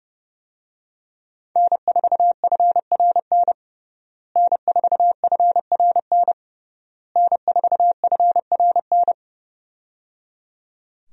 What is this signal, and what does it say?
Morse code, heard as a keyed tone, about 30 wpm, 700 Hz. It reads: N4FRN N4FRN N4FRN
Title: USA Call Signs - 5 Characters - Repeated 3x - 30wpm